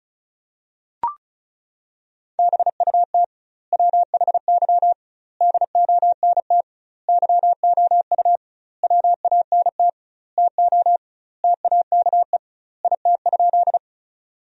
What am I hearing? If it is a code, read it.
Morse code, heard as a keyed tone, about 36 wpm, 700 Hz. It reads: BUT WHY DONT YOU WANT TO TAKE IT?
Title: 2x Word-Spacing - Sentences from Top 300 Words - 35wpm